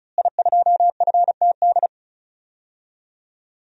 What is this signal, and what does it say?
Morse code, heard as a keyed tone, about 36 wpm, 700 Hz. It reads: I2FTD
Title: DXCC-100 Call Signs - 5 Characters - Repeated 3x - 35wpm